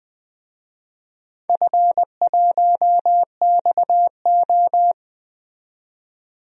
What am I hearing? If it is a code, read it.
Morse code, heard as a keyed tone, about 21 wpm, 700 Hz. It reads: F1XO
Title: DXCC-100 Call Signs - 4 Characters - 20wpm